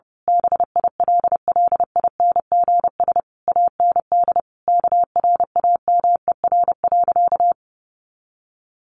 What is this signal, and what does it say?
Morse code, heard as a keyed tone, about 30 wpm, 700 Hz. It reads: BILLINGS AND KRAMER.
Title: A Sound Of Thunder 30wpm